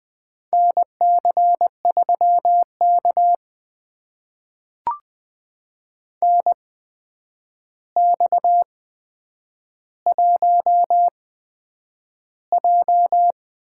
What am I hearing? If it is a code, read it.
Morse code, heard as a keyed tone, about 20 wpm, spaced as for 9 wpm, 700 Hz. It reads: NC3K NX1J
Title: USA Call Signs - 4 Characters - 8x Character Spacing - 20wpm